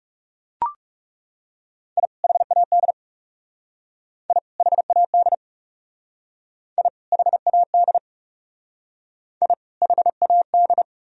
Morse code, heard as a keyed tone, about 36 wpm, 700 Hz. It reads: I HAD I HAD I HAD I HAD
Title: Speed-Racing - Sets of 2 Words 30wpm